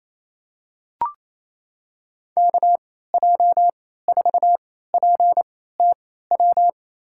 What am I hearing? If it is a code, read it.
Morse code, heard as a keyed tone, about 28 wpm, spaced as for 16 wpm, 700 Hz. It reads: KJ4PTW
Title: USA Call Signs - 6 Characters - 3x Character Spacing - 28wpm